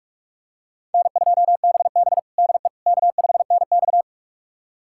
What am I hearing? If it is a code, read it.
Morse code, heard as a keyed tone, about 41 wpm, 700 Hz. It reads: N2BB DE K5NX
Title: USA Callsign DE Callsign - 4 Characters - 45wpm